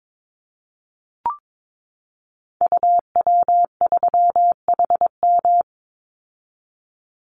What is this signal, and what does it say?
Morse code, heard as a keyed tone, about 22 wpm, 700 Hz. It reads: UW3HM
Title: Call Signs - 22wpm